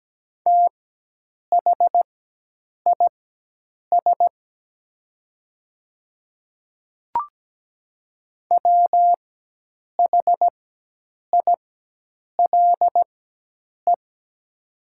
Morse code, heard as a keyed tone, about 17 wpm, spaced as for 8 wpm, 700 Hz. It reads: THIS WHILE
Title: 4x Character Spacing - Top 300 Words - 17wpm